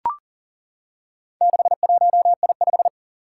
Morse code, heard as a keyed tone, about 40 wpm, 700 Hz. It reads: 61I5
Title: Four Challenging Letter-Number Combinations 40wpm